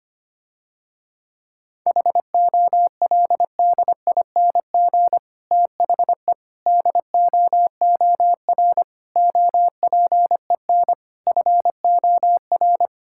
Morse code, heard as a keyed tone, about 25 wpm, 700 Hz. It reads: HOLDING THE DOOR OPEN FOR
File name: practice / 25wpm / Difficult Sentences with Hints - 25wpm